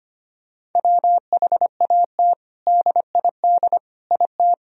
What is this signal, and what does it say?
Morse code, heard as a keyed tone, about 25 wpm, 700 Hz. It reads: WHAT DID IT
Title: Sentences from Top 1000 Words - Review Part 3 - 25wpm